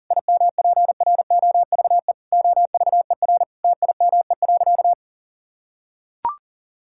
Morse code, heard as a keyed tone, about 40 wpm, 690 Hz. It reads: IMPROVE OVER TIME.
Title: Sentences from Top 1000 Words - Review Part 5 - 40wpm